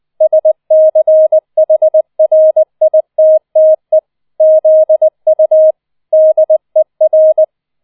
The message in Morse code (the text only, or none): SCHRITTE ZU DER